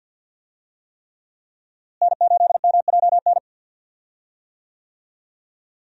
Morse code, heard as a keyed tone, about 48 wpm, 700 Hz. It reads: N8MJN
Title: USA Call Signs - 5 Characters - Repeated 3x - 50wpm